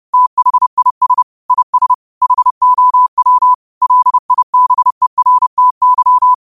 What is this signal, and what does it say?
Morse code, heard as a keyed tone, about 30 wpm, 1000 Hz. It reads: THIS IS HOW LIBERTY
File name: qotdfiles / 30wpm